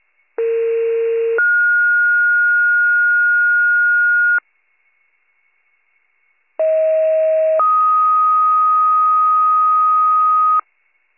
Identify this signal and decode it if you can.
Two-tone page, heard as a one-second tone, then a three-second tone, about 460 then 1430 Hz, about 640 then 1250 Hz. It is heard from 0.4 to 10.6 s.